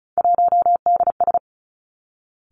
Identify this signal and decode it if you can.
Morse code, heard as a keyed tone, about 36 wpm, 700 Hz. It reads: 1DS
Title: Three Challenging Letter-Number Combinations 35wpm